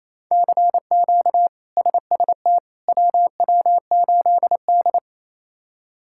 Morse code, heard as a keyed tone, about 28 wpm, 700 Hz. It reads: CQ SST WW8D
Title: K1USN SST QSO Practice - 28wpm